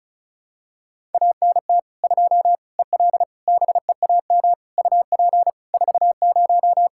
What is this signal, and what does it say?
Morse code, heard as a keyed tone, about 34 wpm, 700 Hz. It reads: ANT 2 EL BEAM UP 40
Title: K1USN SST QSO Practice 35wpm